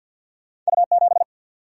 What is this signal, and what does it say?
Morse code, heard as a keyed tone, about 52 wpm, 700 Hz. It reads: U 7
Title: Single-Letter Single-Number 50wpm